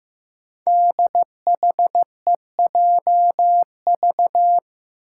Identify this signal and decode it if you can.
Morse code, heard as a keyed tone, about 15 wpm, 700 Hz. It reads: DHEJV
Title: Five Challenging Letter-Number Combinations 15wpm